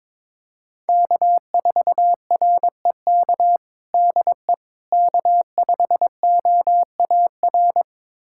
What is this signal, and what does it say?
Morse code, heard as a keyed tone, about 22 wpm, 700 Hz. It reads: K4REK DE K5OAR
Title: USA Callsign DE Callsign - 5 Characters - 22wpm